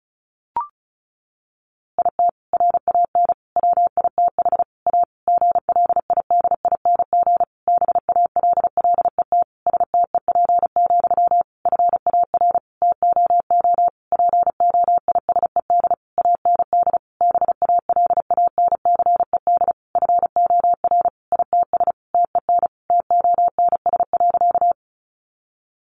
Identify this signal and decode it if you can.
Morse code, heard as a keyed tone, about 34 wpm, 700 Hz. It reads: IT RAN WITH A GLIDING BALLET STEP, FAR TOO POISED AND BALANCED FOR ITS TEN TONS.
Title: A Sound Of Thunder 35wpm